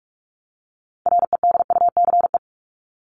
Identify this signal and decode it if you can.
Morse code, heard as a keyed tone, about 44 wpm, 700 Hz. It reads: REDUCE
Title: ED Suffix - Top 100 Words with Base Words 45wpm